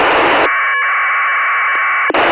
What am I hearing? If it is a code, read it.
Packet radio (AX.25, as in APRS).
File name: ISS-01